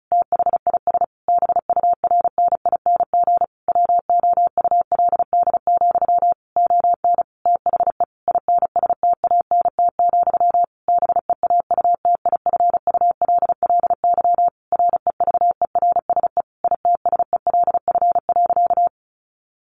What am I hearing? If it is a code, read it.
Morse code, heard as a keyed tone, about 34 wpm, 700 Hz. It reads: THIS BURNING WOULD, ON THE INSTANT, BEAUTIFULLY REVERSE ITSELF.
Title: A Sound Of Thunder 35wpm